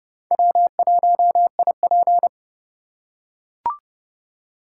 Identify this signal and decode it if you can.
Morse code, heard as a keyed tone, about 30 wpm, 700 Hz. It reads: W1IP
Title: USA Call Signs - 4 Characters - 2x Character Spacing - 30wpm